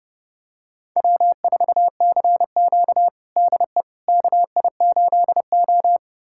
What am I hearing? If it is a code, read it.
Morse code, heard as a keyed tone, about 29 wpm, 700 Hz. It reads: W4CQ DE KI8O
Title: USA Callsign DE Callsign - 4 Characters - 30wpm